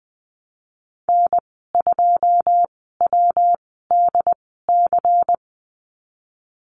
Morse code, heard as a keyed tone, about 20 wpm, spaced as for 14 wpm, 700 Hz. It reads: N2WDC
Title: 193 Lesson 31g - Letter v - Call Signs 20wpm